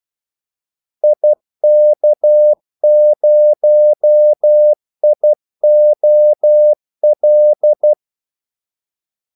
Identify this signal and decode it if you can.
Morse code, heard as a keyed tone, about 12 wpm, 600 Hz. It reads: IK0IOL
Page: Kaldesignaler 12 wpm